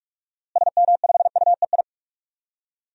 Morse code, heard as a keyed tone, about 46 wpm, 700 Hz. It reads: SM5UEI
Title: DXCC-100 Call Signs - 6 Characters - Repeated 2x - 45wpm